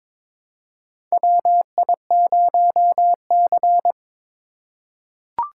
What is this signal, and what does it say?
Morse code, heard as a keyed tone, about 22 wpm, 700 Hz. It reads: WI0C
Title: USA Call Signs - 4 Characters - 4x Character Spacing - 22wpm